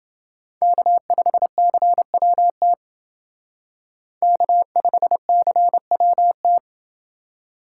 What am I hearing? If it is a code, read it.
Morse code, heard as a keyed tone, about 28 wpm, 700 Hz. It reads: K5CWT K5CWT